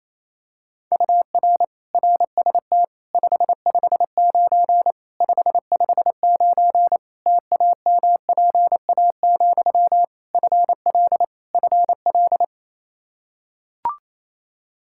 Morse code, heard as a keyed tone, about 28 wpm, 700 Hz. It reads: UR RST 559 559 TAMPA, FL FL
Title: Practice QSO - Exchange 1 - Beginning - 28wpm